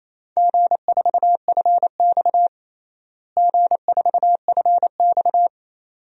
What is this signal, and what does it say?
Morse code, heard as a keyed tone, about 27 wpm, 700 Hz. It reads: G4FX G4FX
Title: DXCC-100 Call Signs - 4 Characters - Repeated 2x - 28wpm